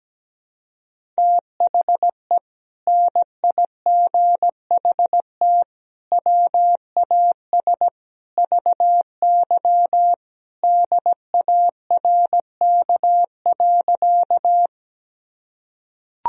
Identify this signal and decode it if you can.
Morse code, heard as a keyed tone, about 17 wpm, 700 Hz. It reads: THE NIGHT WAS VY DARK.
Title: Sentences from Top 500 Words and Abbreviations 17wpm